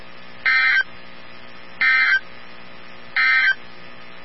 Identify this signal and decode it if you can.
Emergency Alert System (SAME) tones, heard as harsh buzzing bursts. It is heard from 0.3 to 3.6 s.
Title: NOAA Weather Radio 162.550